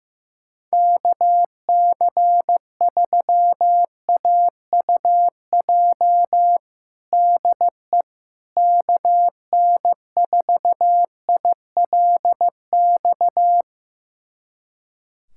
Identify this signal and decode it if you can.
Morse code, heard as a keyed tone, about 15 wpm, 700 Hz. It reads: KC3AUJ DE KN4ILX
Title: USA Callsign DE Callsign - 6 Characters - 15wpm